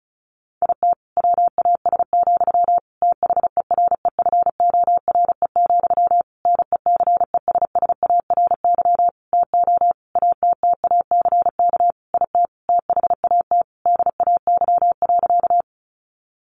Morse code, heard as a keyed tone, about 34 wpm, 700 Hz. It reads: IT WAS, THEREFORE, NECESSARY TO ATTACK IT THAT DAY.
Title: Sentences from Top 1500 Words 35wpm